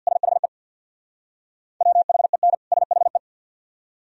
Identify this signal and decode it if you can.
Morse code, heard as a keyed tone, about 66 wpm, 700 Hz. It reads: SHE WHEN SHE